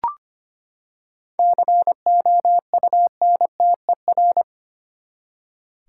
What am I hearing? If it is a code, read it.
Morse code, heard as a keyed tone, about 25 wpm, 700 Hz. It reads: COUNTER
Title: Seven Letter Words - 25wpm